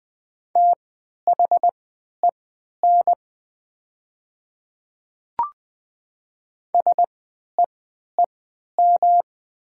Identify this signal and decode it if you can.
Morse code, heard as a keyed tone, about 20 wpm, spaced as for 11 wpm, 690 Hz. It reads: THEN SEEM
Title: 3x Character Spacing - Top 500 Words - 20wpm